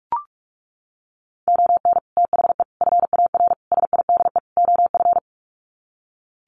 Morse code, heard as a keyed tone, about 44 wpm, 700 Hz. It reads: ON THE FAR SIDE OF
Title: Sets of 5 Words 45wpm